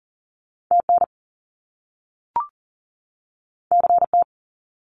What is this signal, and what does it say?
Morse code, heard as a keyed tone, about 41 wpm, 700 Hz. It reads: TN CT
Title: US States 40wpm